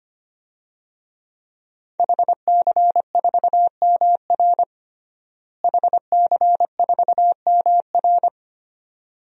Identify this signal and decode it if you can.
Morse code, heard as a keyed tone, about 25 wpm, 700 Hz. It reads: HC4MR HC4MR
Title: DXCC-100 Call Signs - 5 Characters - Repeated 2x - 25wpm